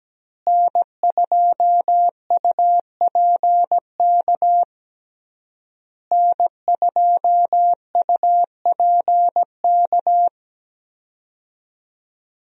Morse code, heard as a keyed tone, about 17 wpm, 700 Hz. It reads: N2UPK N2UPK
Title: USA Call Signs - 5 Characters - Repeated 3x - 17wpm